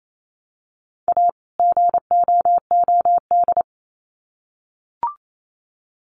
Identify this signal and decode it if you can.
Morse code, heard as a keyed tone, about 28 wpm, 700 Hz. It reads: A GOOD
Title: Sets of 2 Words 28wpm